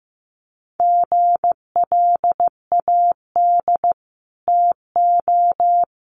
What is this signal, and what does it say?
Morse code, heard as a keyed tone, about 15 wpm, 700 Hz. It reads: GLAD TO
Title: Sentences from Top 800 Words 15wpm